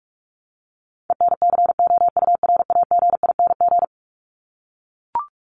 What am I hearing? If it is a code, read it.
Morse code, heard as a keyed tone, about 44 wpm, 700 Hz. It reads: ENCOURAGING
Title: Top 1751-2000 Words 45wpm